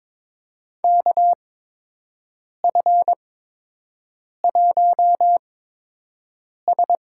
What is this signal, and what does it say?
Morse code, heard as a keyed tone, about 22 wpm, spaced as for 6 wpm, 700 Hz. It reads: KF1S
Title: USA Call Signs - 4 Characters - 8x Character Spacing - 22wpm